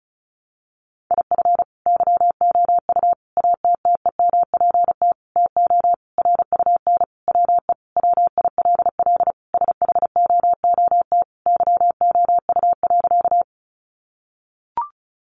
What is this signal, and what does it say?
Morse code, heard as a keyed tone, about 34 wpm, 700 Hz. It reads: IF YOU ATTEMPT TO RUN WE WILL SHOOT YOU.
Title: Sentences from Top 1200 Words 35wpm